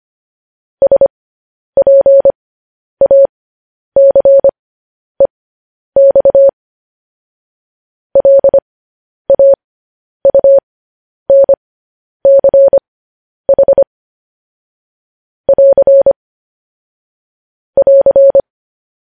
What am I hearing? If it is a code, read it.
Morse code, heard as a keyed tone, about 25 wpm, spaced as for 10 wpm, 550 Hz. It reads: SPACEX LAUNCH <AR> <AR>
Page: news-headlines-in-morse-code-at-10-wpm